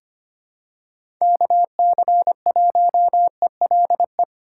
Morse code, heard as a keyed tone, about 25 wpm, 700 Hz. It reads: KC1ELE
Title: USA Call Signs - 6 Characters - 3x Character Spacing - 25wpm